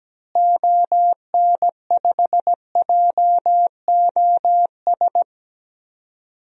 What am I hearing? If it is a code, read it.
Morse code, heard as a keyed tone, about 17 wpm, 700 Hz. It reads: ON5JOS
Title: DXCC-100 Call Signs - 6 Characters - Repeated 2x - 17wpm